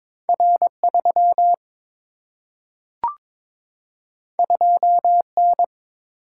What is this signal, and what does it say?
Morse code, heard as a keyed tone, about 22 wpm, 700 Hz. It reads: R 3 2 N
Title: Single-Letter Single-Number 22wpm